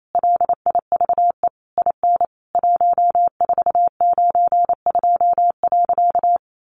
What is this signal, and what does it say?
Morse code, heard as a keyed tone, about 28 wpm, 700 Hz. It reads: LIVE IN 1492.
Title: A Sound Of Thunder 28wpm